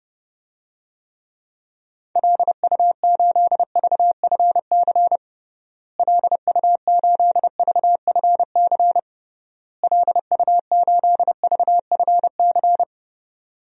Morse code, heard as a keyed tone, about 32 wpm, 700 Hz. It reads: LU8VFC LU8VFC LU8VFC
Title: DXCC-100 Call Signs - 6 Characters - Repeated 3x - 30wpm